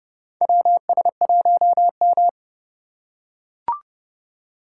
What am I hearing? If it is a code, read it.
Morse code, heard as a keyed tone, about 29 wpm, 700 Hz. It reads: WS1M
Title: USA Call Signs - 4 Characters - 6x Character Spacing - 30wpm